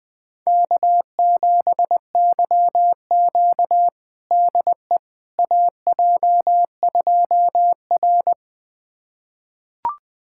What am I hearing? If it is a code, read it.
Morse code, heard as a keyed tone, about 20 wpm, 700 Hz. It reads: K7YQ DE AJ2R
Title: USA Callsign DE Callsign - 4 Characters - 20wpm